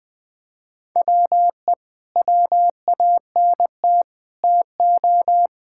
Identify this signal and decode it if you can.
Morse code, heard as a keyed tone, about 19 wpm, 700 Hz. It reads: WE WANT TO
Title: Sets of 8 Words - 20wpm